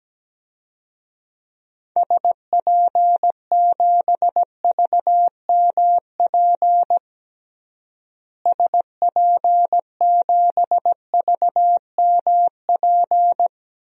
Morse code, heard as a keyed tone, about 17 wpm, 700 Hz. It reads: SP7VMP SP7VMP
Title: DXCC-100 Call Signs - 6 Characters - Repeated 3x - 17wpm